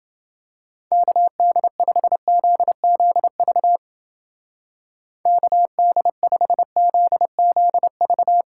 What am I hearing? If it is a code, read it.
Morse code, heard as a keyed tone, about 29 wpm, 700 Hz. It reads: KD5ZZV KD5ZZV